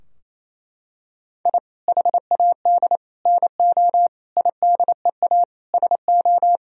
Morse code, heard as a keyed tone, about 27 wpm, 700 Hz. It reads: I HAD NO IDEA SO
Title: Sentences from Top 1000 Words - Review Part 4 - 28wpm